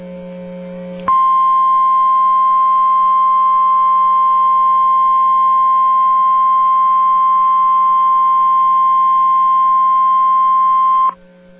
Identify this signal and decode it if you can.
Emergency Alert System (SAME) tones (the 1050 Hz attention tone). It is heard from 1.1 to 11.2 s.